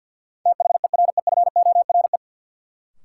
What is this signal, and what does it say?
Morse code, heard as a keyed tone, about 53 wpm, 700 Hz. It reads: THEREFORE
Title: Sentences from Top 800 Words - New Word Drill - 50wpm